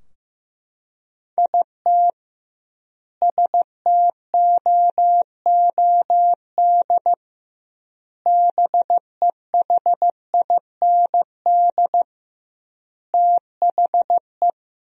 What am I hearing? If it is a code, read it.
Morse code, heard as a keyed tone, about 15 wpm, 700 Hz. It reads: IT STOOD BEHIND THE
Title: 2x Word-Spacing - Sentences from Top 500 Words - 15wpm